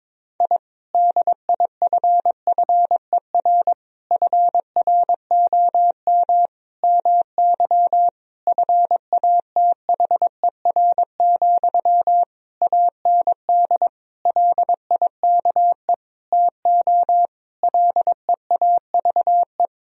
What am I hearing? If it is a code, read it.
Morse code, heard as a keyed tone, about 22 wpm, 700 Hz. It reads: I DIFFER FROM MY FATHER, AND LIKE TO LEAVE
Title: Sentences from Top 1000 Words - Review Part 1 - 22wpm